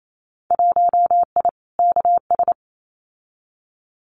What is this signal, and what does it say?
Morse code, heard as a keyed tone, about 28 wpm, 690 Hz. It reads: 1I KS
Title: Winter Field Day Exchanges Encore 28wpm